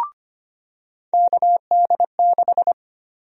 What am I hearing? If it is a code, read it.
Morse code, heard as a keyed tone, about 25 wpm, 700 Hz. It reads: KD6
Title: USA Call Sign Prefixes - 25wpm